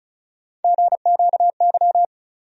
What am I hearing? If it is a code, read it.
Morse code, heard as a keyed tone, about 36 wpm, 700 Hz. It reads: GQY
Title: USA Call Sign Postfixes - 3 Letters 35wpm